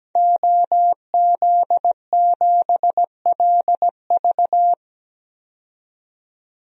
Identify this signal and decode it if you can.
Morse code, heard as a keyed tone, about 17 wpm, 700 Hz. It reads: OZ7LV
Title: DXCC-100 Call Signs - 5 Characters - 17wpm